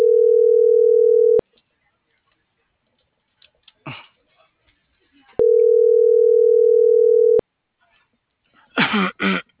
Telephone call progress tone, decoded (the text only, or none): ringback tone